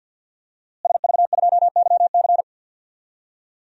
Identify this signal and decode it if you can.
Morse code, heard as a keyed tone, about 47 wpm, 700 Hz. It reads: SV2YC
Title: DXCC-100 Call Signs - 5 Characters - Repeated 3x - 50wpm